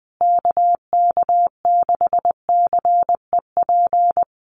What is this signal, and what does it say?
Morse code, heard as a keyed tone, about 20 wpm, 700 Hz. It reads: KK6CEP
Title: USA Call Signs Repeated 3x 20wpm